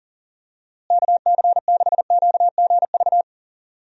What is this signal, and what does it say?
Morse code, heard as a keyed tone, about 40 wpm, 700 Hz. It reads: KC6QGV
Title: USA Call Signs - 6 Characters - Repeated 3x - 40wpm